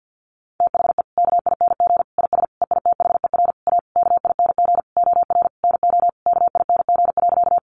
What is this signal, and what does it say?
Morse code, heard as a keyed tone, about 50 wpm, 700 Hz. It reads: THE KING IS EITHER A KING OR NO KING.
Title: Sentences from Top 600 Words 50wpm